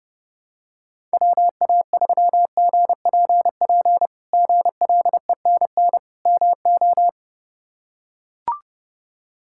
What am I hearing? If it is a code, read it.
Morse code, heard as a keyed tone, about 30 wpm, 700 Hz. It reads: WA3GPP GLENN MO
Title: K1USN SST QSO Practice - 30wpm